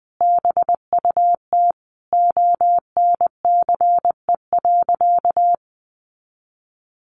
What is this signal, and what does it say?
Morse code, heard as a keyed tone, about 20 wpm, 700 Hz. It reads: BUT ONCE.
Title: Sentences from Top 1300 Words 20wpm